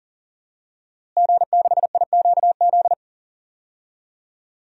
Morse code, heard as a keyed tone, about 37 wpm, 700 Hz. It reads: G6IQZ
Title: DXCC-100 Call Signs - 5 Characters - Repeated 2x - 40wpm